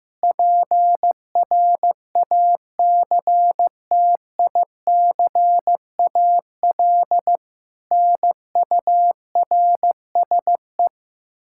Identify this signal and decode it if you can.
Morse code, heard as a keyed tone, about 15 wpm, 700 Hz. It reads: PRACTICAL NURSE
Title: Practice QSO - Exchange 3 - More Info - 15wpm